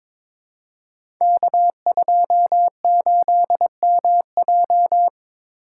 Morse code, heard as a keyed tone, about 22 wpm, 700 Hz. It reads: K28MJ
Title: Code Groups - 5 Letter:Numbers - 22wpm